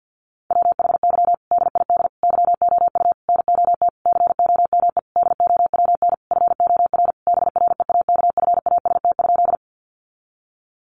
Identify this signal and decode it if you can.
Morse code, heard as a keyed tone, about 50 wpm, 700 Hz. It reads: WHY DID YOU NOT COME DOWN FOR BREAKFAST?
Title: Sentences from Top 1500 Words 50wpm